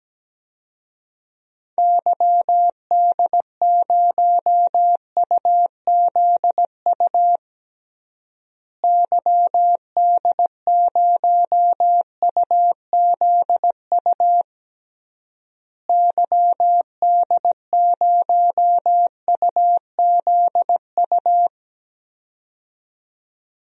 Morse code, heard as a keyed tone, about 17 wpm, 700 Hz. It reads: YD0UZU YD0UZU YD0UZU
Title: DXCC-100 Call Signs - 6 Characters - Repeated 3x - 17wpm